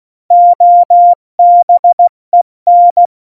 Morse code, heard as a keyed tone, about 16 wpm, 700 Hz. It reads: OBEN
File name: Morsekurs / Witze / Witz 14. Januar 2020, 70 F80